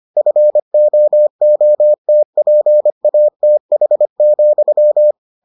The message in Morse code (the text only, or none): FOOTPATH,